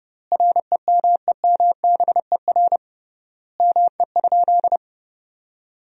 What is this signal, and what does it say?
Morse code, heard as a keyed tone, about 30 wpm, 700 Hz. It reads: REMEMBER ME?
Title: 3x Word-Spacing - Sentences from Top 400 Words - 30wpm